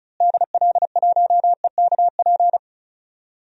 Morse code, heard as a keyed tone, about 36 wpm, 700 Hz. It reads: DL1EKP
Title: DXCC-100 Call Signs - 6 Characters - Repeated 2x - 35wpm